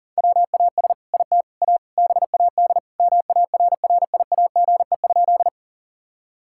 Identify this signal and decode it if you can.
Morse code, heard as a keyed tone, about 37 wpm, 700 Hz. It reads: WAS IT A BAD MARRIAGE?
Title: Sentences from Top 1000 Words - Review Part 5 - 40wpm